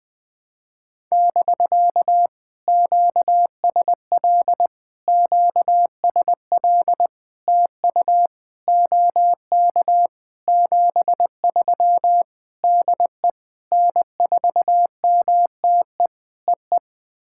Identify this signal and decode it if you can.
Morse code, heard as a keyed tone, about 20 wpm, 700 Hz. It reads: <BK> QSL QSL TU OK 73 DE N4MTE EE
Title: POTA QSO Practice - 20wpm